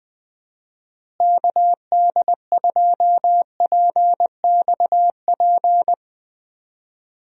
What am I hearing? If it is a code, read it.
Morse code, heard as a keyed tone, about 20 wpm, 700 Hz. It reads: KD2PXP
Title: USA Call Signs - 6 Characters - Repeated 3x - 20wpm